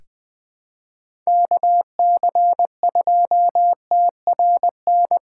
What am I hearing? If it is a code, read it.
Morse code, heard as a keyed tone, about 20 wpm, 700 Hz. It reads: KC2TRN
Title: USA Call Signs - 6 Characters - 3x Character Spacing - 20wpm